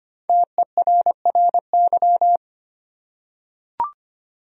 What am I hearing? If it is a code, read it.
Morse code, heard as a keyed tone, about 25 wpm, 700 Hz. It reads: TERRY
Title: USA Common Names - Max 5 Letters - 25wpm